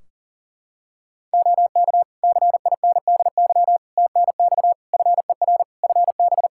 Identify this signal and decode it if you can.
Morse code, heard as a keyed tone, about 40 wpm, 700 Hz. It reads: OK CINDY TNX FER FB
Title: K1USN SST QSO Practice 40wpm